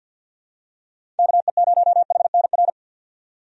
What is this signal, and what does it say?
Morse code, heard as a keyed tone, about 49 wpm, 700 Hz. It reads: KE0HNR
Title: USA Call Signs - 6 Characters - 50wpm